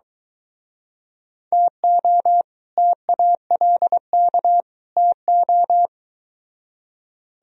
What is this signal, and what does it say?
Morse code, heard as a keyed tone, about 24 wpm, 700 Hz. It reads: TO TALK TO